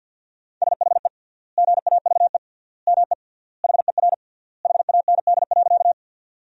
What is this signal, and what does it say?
Morse code, heard as a keyed tone, about 50 wpm, 700 Hz. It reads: SHE GAVE ME HER HAND.
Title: 3x Word-Spacing - Sentences from Top 500 Words - 50wpm